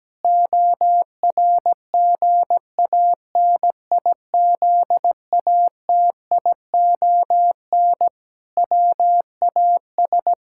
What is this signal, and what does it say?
Morse code, heard as a keyed tone, about 17 wpm, 700 Hz. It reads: ORGANIZATION WAS
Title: Sentences from Top 1000 Words - Review Part 5 - 17wpm